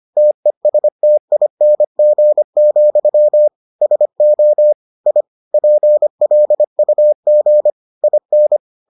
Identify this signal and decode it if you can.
Morse code, heard as a keyed tone, about 25 wpm, 600 Hz. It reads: TESTING, SO I PLUG IN